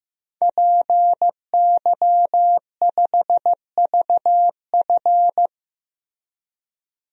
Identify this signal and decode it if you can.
Morse code, heard as a keyed tone, about 15 wpm, 700 Hz. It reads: PY5VF
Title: DXCC-100 Call Signs - 5 Characters - Repeated 3x - 15wpm